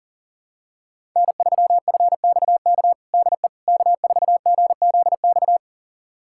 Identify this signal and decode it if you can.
Morse code, heard as a keyed tone, about 40 wpm, 700 Hz. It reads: N3FXK DE K4GZX
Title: USA Callsign DE Callsign - 5 Characters - 40wpm